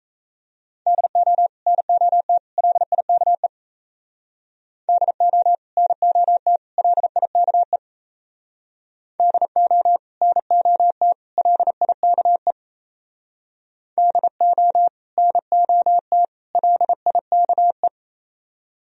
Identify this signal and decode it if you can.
Morse code, heard as a keyed tone, about 35 wpm, 700 Hz. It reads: DO NOT LIKE DO NOT LIKE DO NOT LIKE DO NOT LIKE
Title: Speed-Racing - Sets of 3 Words 28wpm